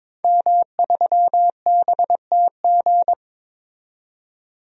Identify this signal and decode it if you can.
Morse code, heard as a keyed tone, about 22 wpm, 700 Hz. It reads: M3BTG
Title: DXCC-100 Call Signs - 5 Characters - 22wpm